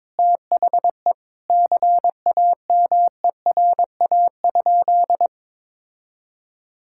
Morse code, heard as a keyed tone, about 22 wpm, 700 Hz. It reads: THE CAMERA?
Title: Sentences from Top 1000 Words - Review Part 5 - 22wpm